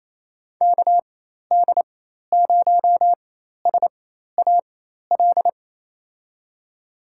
Morse code, heard as a keyed tone, about 28 wpm, spaced as for 13 wpm, 700 Hz. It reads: KD0SAL
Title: USA Call Signs - 6 Characters - 4x Character Spacing - 28wpm